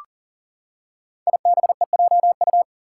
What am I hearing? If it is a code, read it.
Morse code, heard as a keyed tone, about 43 wpm, 700 Hz. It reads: IBEJU
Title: Five Challenging Letter-Number Combinations 40wpm